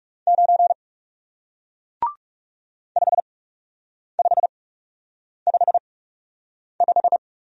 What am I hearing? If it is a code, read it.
Morse code, heard as a keyed tone, about 39 wpm, spaced as for 8 wpm, 700 Hz. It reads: 9 5555